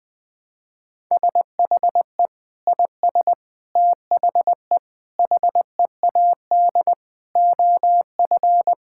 Morse code, heard as a keyed tone, about 20 wpm, 700 Hz. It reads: SHE IS THE HEAD OF
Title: Sentences with 3-Word Phrases - Spoken Beginning - 20wpm